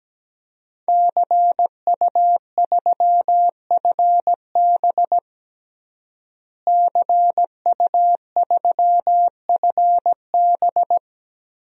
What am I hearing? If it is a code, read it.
Morse code, heard as a keyed tone, about 17 wpm, 700 Hz. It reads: CU3FB CU3FB
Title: DXCC-100 Call Signs - 5 Characters - Repeated 2x - 17wpm